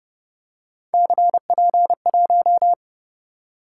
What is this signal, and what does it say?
Morse code, heard as a keyed tone, about 31 wpm, 700 Hz. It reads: CP1